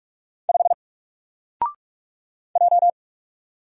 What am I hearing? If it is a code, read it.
Morse code, heard as a keyed tone, about 44 wpm, 700 Hz. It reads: 5 J
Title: Single Character (Letter:Number:Punctuation:Prosign) Alt 45wpm